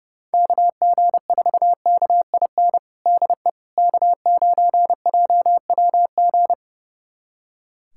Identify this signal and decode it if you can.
Morse code, heard as a keyed tone, about 32 wpm, 700 Hz. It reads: KG4KIN DE K9JWG
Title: USA Callsign DE Callsign 30wpm